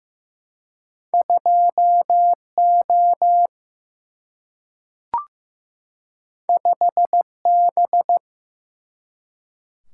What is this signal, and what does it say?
Morse code, heard as a keyed tone, about 15 wpm, 700 Hz. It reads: 2 O 5 B
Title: Single-Letter Single-Number 15wpm